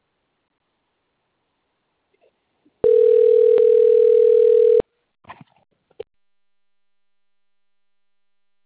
Ringback tone (North America), 440 + 480 Hz.